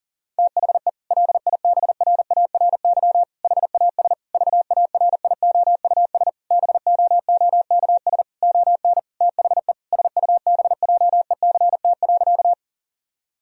Morse code, heard as a keyed tone, about 37 wpm, 700 Hz. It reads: THE LIBRARY HAS VARIOUS BOOKS ON THE SUBJECT.
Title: Sentences from Top 1000 Words - Review Part 4 - 40wpm